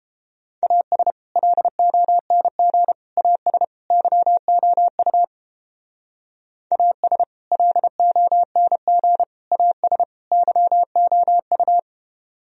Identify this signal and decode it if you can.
Morse code, heard as a keyed tone, about 30 wpm, 700 Hz. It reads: AS LONG AS YOU AS LONG AS YOU